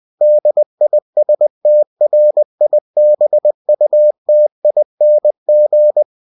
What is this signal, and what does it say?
Morse code, heard as a keyed tone, about 20 wpm, 600 Hz. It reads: DISTRIBUTING
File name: cw-books / winnie-the-pooh / winnie-20wpm-page-0116